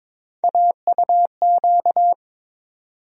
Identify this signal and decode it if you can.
Morse code, heard as a keyed tone, about 22 wpm, 700 Hz. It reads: AUQ
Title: USA Call Sign Postfixes - 3 Letters 22wpm